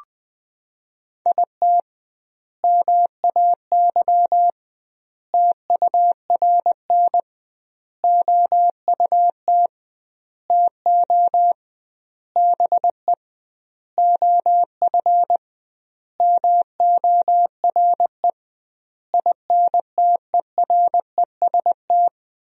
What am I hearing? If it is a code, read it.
Morse code, heard as a keyed tone, about 21 wpm, 700 Hz. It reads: IT MAY TURN OUT TO BE OF MORE INTEREST
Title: 2x Word-Spacing - Sentences from Top 400 Words - 20wpm